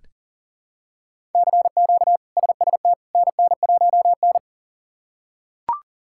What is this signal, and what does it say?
Morse code, heard as a keyed tone, about 40 wpm, 700 Hz. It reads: CQ SST NN1N
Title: K1USN SST QSO Practice - 40wpm